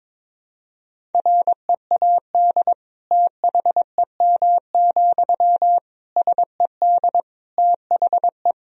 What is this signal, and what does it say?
Morse code, heard as a keyed tone, about 22 wpm, 700 Hz. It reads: READ THEM, SED THE
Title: Sentences from Top 400 Words and Abbreviations 22wpm